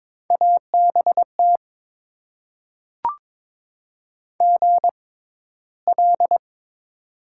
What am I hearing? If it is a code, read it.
Morse code, heard as a keyed tone, about 22 wpm, spaced as for 14 wpm, 700 Hz. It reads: ABT GL
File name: Top 90 QSO Elements - 6x Character Spacing - 22wpm